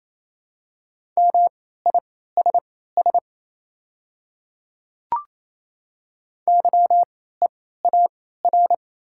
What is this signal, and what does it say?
Morse code, heard as a keyed tone, about 28 wpm, spaced as for 16 wpm, 700 Hz. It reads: MISS YEAR
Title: 3x Character Spacing - Top 500 Words - 28wpm